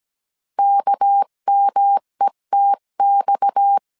Morse code, heard as a keyed tone, about 17 wpm, 800 Hz. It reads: KMETX
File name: CW / DET / 1 6 1500C S C 16wpm 2733